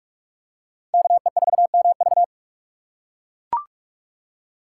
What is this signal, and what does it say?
Morse code, heard as a keyed tone, about 43 wpm, 700 Hz. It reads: KE4MV